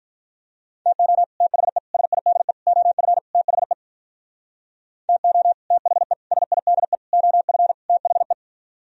Morse code, heard as a keyed tone, about 43 wpm, 700 Hz. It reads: TO THE SIDE OF THE TO THE SIDE OF THE
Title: Speed-Racing - Sets of 5 Words 35wpm